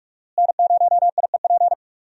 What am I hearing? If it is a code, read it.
Morse code, heard as a keyed tone, about 46 wpm, 700 Hz. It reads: N0IEP
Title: USA Call Signs - 5 Characters - Repeated 3x - 45wpm